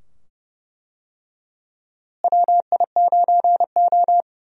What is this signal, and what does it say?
Morse code, heard as a keyed tone, about 30 wpm, 700 Hz. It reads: WI9O
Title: USA Call Signs - 4 Characters - Repeated 2x - 30wpm